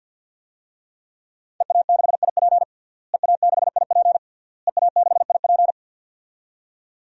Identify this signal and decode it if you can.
Morse code, heard as a keyed tone, about 51 wpm, 700 Hz. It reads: EA6IP EA6IP EA6IP
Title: DXCC-100 Call Signs - 5 Characters - Repeated 3x - 50wpm